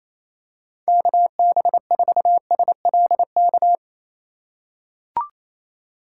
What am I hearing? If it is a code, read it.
Morse code, heard as a keyed tone, about 28 wpm, 700 Hz. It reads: KB4SLK
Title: USA Call Signs - 6 Characters - 8x Character Spacing - 28wpm